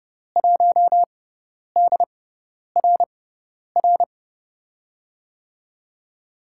Morse code, heard as a keyed tone, about 30 wpm, spaced as for 10 wpm, 700 Hz. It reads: 1DRR
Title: USA Call Signs - 6x Character Spacing - 30wpm